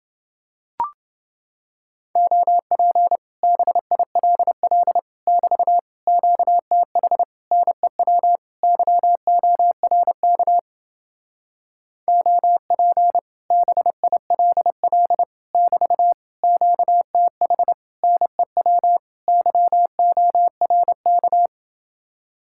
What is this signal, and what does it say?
Morse code, heard as a keyed tone, about 28 wpm, 700 Hz. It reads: OP BILL <BT> QTH NEW YORK OP BILL <BT> QTH NEW YORK